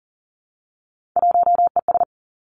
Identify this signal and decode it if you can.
Morse code, heard as a keyed tone, about 41 wpm, 700 Hz. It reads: 1ES
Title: Three Challenging Letter-Number Combinations 40wpm